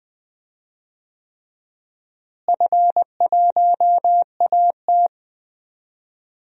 Morse code, heard as a keyed tone, about 20 wpm, 700 Hz. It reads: F1AT
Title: DXCC-100 Call Signs - 4 Characters - Repeated 3x - 20wpm